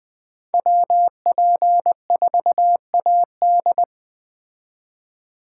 Morse code, heard as a keyed tone, about 20 wpm, 700 Hz. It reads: WP4AD
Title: Speed Racing - USA 2x2 Call Signs 17wpm